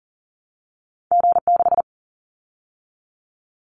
Morse code, heard as a keyed tone, about 40 wpm, 700 Hz. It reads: G6
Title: Callsign Prefixes for Top 25 DXCC - 40wpm